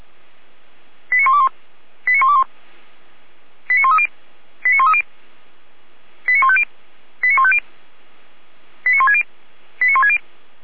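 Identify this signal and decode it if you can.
Five-tone selective call (ZVEI), heard as quick runs of five tones, tones of about 80 ms. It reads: 89121 89121 89130 89130 89150 89150 89160 89160
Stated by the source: ZVEI five-tone call received by SDR.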